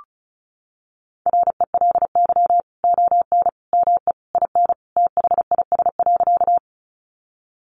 Morse code, heard as a keyed tone, about 34 wpm, 700 Hz. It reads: RELY ON ME IN THIS.
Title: Sentences from Top 1200 Words 35wpm